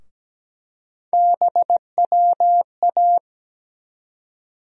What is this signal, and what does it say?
Morse code, heard as a keyed tone, about 17 wpm, 700 Hz. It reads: BWA
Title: USA Call Sign Postfixes - 3 Letters 17wpm